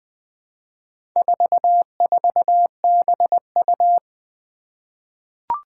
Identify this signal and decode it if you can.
Morse code, heard as a keyed tone, about 20 wpm, 700 Hz. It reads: 44BU
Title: Four Challenging Letter-Number Combinations 20wpm